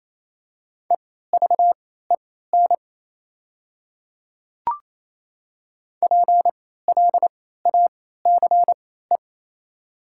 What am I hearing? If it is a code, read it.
Morse code, heard as a keyed tone, about 28 wpm, spaced as for 16 wpm, 690 Hz. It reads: EVEN PLACE
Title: 3x Character Spacing - Top 300 Words - 28wpm